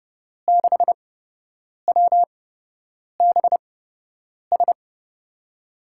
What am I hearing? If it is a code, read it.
Morse code, heard as a keyed tone, about 32 wpm, spaced as for 8 wpm, 700 Hz. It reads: 6WBS
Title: USA Call Signs - 6 Characters - 8x Character Spacing - 30wpm